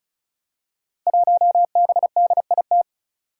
Morse code, heard as a keyed tone, about 36 wpm, 700 Hz. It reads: 1BDIT
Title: Five Challenging Letter-Number Combinations 35wpm